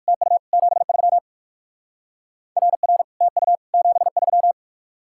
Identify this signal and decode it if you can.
Morse code, heard as a keyed tone, about 60 wpm, 700 Hz. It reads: TU 73 RR TU 73